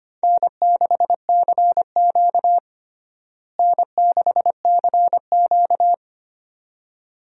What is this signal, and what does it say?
Morse code, heard as a keyed tone, about 25 wpm, 700 Hz. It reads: N6CQ N6CQ
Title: USA Call Signs - 4 Characters - Repeated 2x - 25wpm